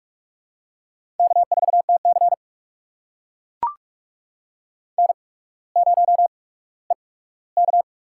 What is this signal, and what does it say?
Morse code, heard as a keyed tone, about 44 wpm, spaced as for 19 wpm, 700 Hz. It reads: K4TC N0EK
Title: USA Call Signs - 4 Characters - 8x Character Spacing - 45wpm